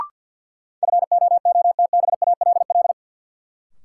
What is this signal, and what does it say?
Morse code, heard as a keyed tone, about 46 wpm, 700 Hz. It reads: FOOTBALL
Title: Sentences from Top 1600 Words - New Word Drill - 50wpm